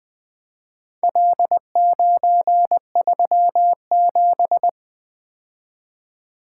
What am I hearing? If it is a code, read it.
Morse code, heard as a keyed tone, about 20 wpm, 700 Hz. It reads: L937
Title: Four Letter-Number Combinations - 20wpm